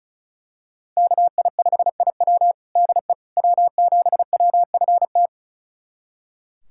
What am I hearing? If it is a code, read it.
Morse code, heard as a keyed tone, about 36 wpm, 700 Hz. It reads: KI5IW DE W7WFT
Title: USA Callsign DE Callsign 35wpm